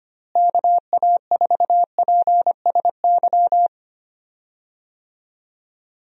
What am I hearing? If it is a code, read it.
Morse code, heard as a keyed tone, about 25 wpm, 700 Hz. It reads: KA4PSY
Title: USA Call Signs - 6 Characters - Repeated 2x - 25wpm